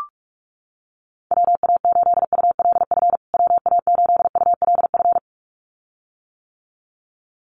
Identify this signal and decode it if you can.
Morse code, heard as a keyed tone, about 44 wpm, 700 Hz. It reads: WA8ULF WA8ULF
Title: USA Call Signs Repeated 2x 45wpm